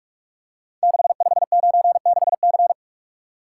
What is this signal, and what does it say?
Morse code, heard as a keyed tone, about 44 wpm, 700 Hz. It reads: 6596C
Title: Code Groups - 5 Letter:Numbers - 45wpm